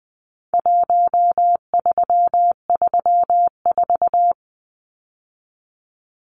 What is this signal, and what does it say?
Morse code, heard as a keyed tone, about 20 wpm, 700 Hz. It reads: 1334
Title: Numbers - Contest Style - 4-Digit 20wpm